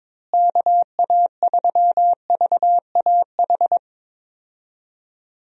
Morse code, heard as a keyed tone, about 22 wpm, 700 Hz. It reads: KA3VAH
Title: USA Call Signs - 6 Characters - Repeated 2x - 22wpm